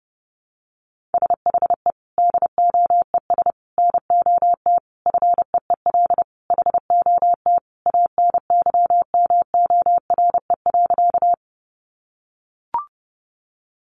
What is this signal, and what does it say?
Morse code, heard as a keyed tone, about 30 wpm, 700 Hz. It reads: SHE DOES NOT FEEL HOT ANYMORE.
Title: Easy Phrases 30wpm